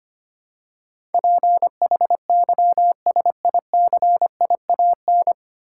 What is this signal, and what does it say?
Morse code, heard as a keyed tone, about 25 wpm, 700 Hz. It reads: PHYSICIAN
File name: practice / 25wpm / 9 Letter Words - First 3 Letters Given - 25wpm